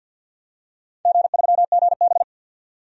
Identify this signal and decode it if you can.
Morse code, heard as a keyed tone, about 51 wpm, 700 Hz. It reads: G3GB
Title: DXCC-100 Call Signs - 4 Characters - 50wpm